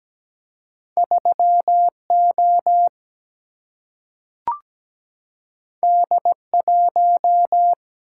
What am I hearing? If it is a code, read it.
Morse code, heard as a keyed tone, about 17 wpm, 700 Hz. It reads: 3O D1
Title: Single-Letter Single-Number 17wpm